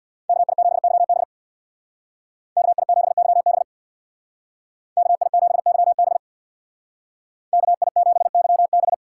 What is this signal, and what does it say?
Morse code, heard as a keyed tone, about 64 wpm, 700 Hz. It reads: KI7YB KI7YB KI7YB KI7YB